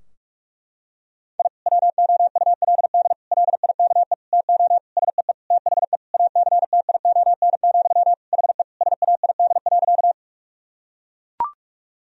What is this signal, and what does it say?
Morse code, heard as a keyed tone, about 41 wpm, 700 Hz. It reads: I WOULD LIKE TO SEE THE ACTION, HE SAID.
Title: Sentences from Top 1000 Words - Review Part 4 - 45wpm